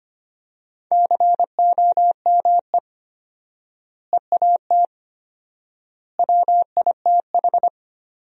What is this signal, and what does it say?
Morse code, heard as a keyed tone, about 25 wpm, 700 Hz. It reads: COME EAT WITH
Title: 4x Word-Spacing - Sentences from Top 300 Words - 25wpm